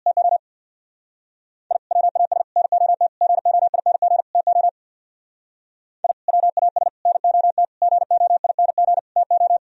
Morse code, heard as a keyed tone, about 63 wpm, 700 Hz. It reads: TO I WAS NOT GOING TO I WAS NOT GOING TO